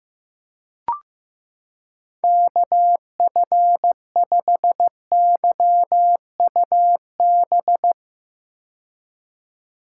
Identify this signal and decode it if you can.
Morse code, heard as a keyed tone, about 16 wpm, 700 Hz. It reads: KF5YUB
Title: K1USN SST QSO Practice - 15wpm